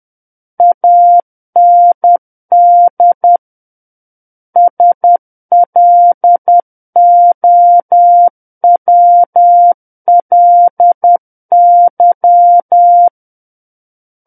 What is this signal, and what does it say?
Morse code, heard as a keyed tone, about 10 wpm, 700 Hz. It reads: AND SLOWLY
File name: Book One / 01-Chapter 1 - 10 WPM